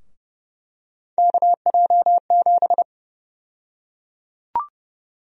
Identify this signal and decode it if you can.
Morse code, heard as a keyed tone, about 29 wpm, 700 Hz. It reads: KJ7
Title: USA Call Sign Prefixes - 30wpm